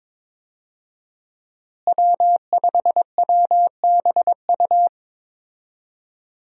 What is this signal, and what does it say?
Morse code, heard as a keyed tone, about 21 wpm, 700 Hz. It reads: W5WBU
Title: USA Call Signs - 5 Characters - Repeated 2x - 22wpm